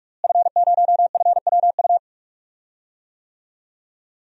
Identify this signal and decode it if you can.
Morse code, heard as a keyed tone, about 44 wpm, 700 Hz. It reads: F0FWU
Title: DXCC-100 Call Signs - 5 Characters - Repeated 2x - 45wpm